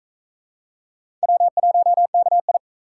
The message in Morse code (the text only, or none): W1KI